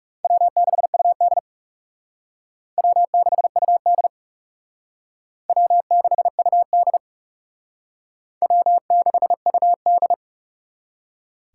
Morse code, heard as a keyed tone, about 37 wpm, 700 Hz. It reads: W6UD W6UD W6UD W6UD
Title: Speed Racing - USA 1x2 Call Signs 30wpm